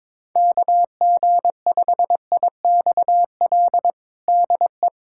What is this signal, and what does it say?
Morse code, heard as a keyed tone, about 22 wpm, 700 Hz. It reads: KG5IXL DE
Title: USA Callsign DE Callsign - 6 Characters - 22wpm